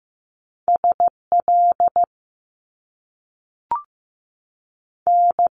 Morse code, heard as a keyed tone, about 15 wpm, 700 Hz. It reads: S L N
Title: Top Two Letter Combinations in Words 15wpm